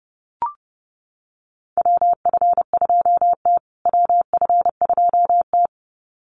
Morse code, heard as a keyed tone, about 30 wpm, 700 Hz. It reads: WF2T WF2T
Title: USA Call Signs Repeated 2x 30wpm